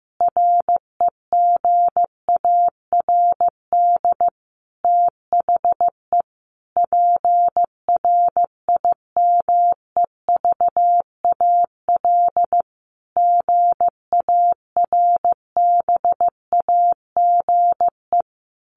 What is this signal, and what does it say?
Morse code, heard as a keyed tone, about 15 wpm, 700 Hz. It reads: REGARD THE PRIMEVAL GARBAGE
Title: A Sound Of Thunder 15wpm